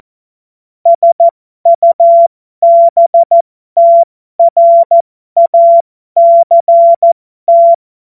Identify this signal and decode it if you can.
Morse code, heard as a keyed tone, about 14 wpm, spaced as for 12 wpm, 680 Hz. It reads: SUBTRACT
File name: Morsekurs / Witze